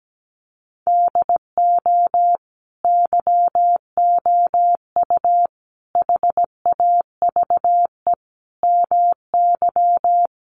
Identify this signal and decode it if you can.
Morse code, heard as a keyed tone, about 17 wpm, 700 Hz. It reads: DO YOU HAVE MY
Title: Sentences from Top 500 Words 17wpm